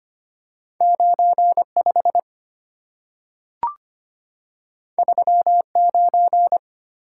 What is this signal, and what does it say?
Morse code, heard as a keyed tone, about 25 wpm, 700 Hz. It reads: 95 39
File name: Two Letter-Number Combinations - 25wpm